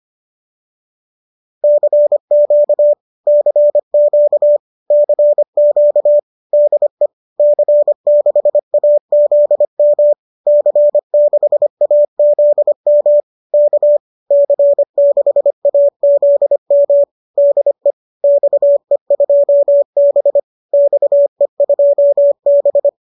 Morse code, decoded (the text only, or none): CQ CQ CQ DE C6AZM C6AZM K C6AZM DE XE2B XE2B